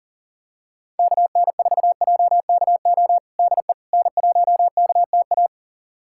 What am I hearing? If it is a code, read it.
Morse code, heard as a keyed tone, about 37 wpm, 700 Hz. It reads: KN4JKO DE N1KTA
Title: USA Callsign DE Callsign 40wpm